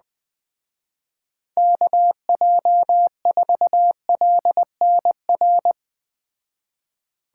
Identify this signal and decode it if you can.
Morse code, heard as a keyed tone, about 21 wpm, 700 Hz. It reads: KJ4LNR
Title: Speed Racing - USA 2x3 Call Signs 17wpm